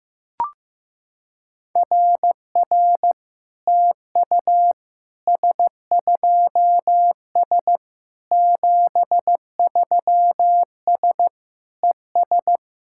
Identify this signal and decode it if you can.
Morse code, heard as a keyed tone, about 14 wpm, 700 Hz. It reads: RR TU S2S 73S ES
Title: SOTA QSO Practice 15wpm